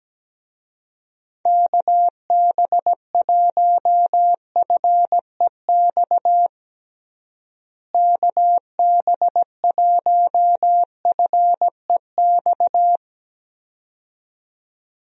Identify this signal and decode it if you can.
Morse code, heard as a keyed tone, about 17 wpm, 700 Hz. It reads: KB1FEX KB1FEX
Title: USA Call Signs - 6 Characters - Repeated 2x - 17wpm